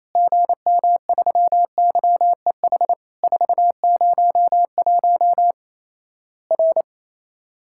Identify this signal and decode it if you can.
Morse code, heard as a keyed tone, about 28 wpm, 700 Hz. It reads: GM3YEH 401 R
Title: Worked All Europe DX Contest - QTC Exchange - 28wpm